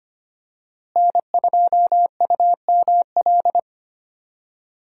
Morse code, heard as a keyed tone, about 25 wpm, 700 Hz. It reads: N2UML
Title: USA Call Signs - 5 Characters - Repeated 3x - 25wpm